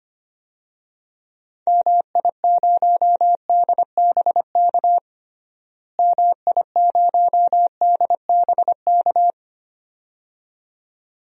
Morse code, heard as a keyed tone, about 25 wpm, 700 Hz. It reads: MI0DBK MI0DBK
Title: DXCC-100 Call Signs - 6 Characters - Repeated 2x - 25wpm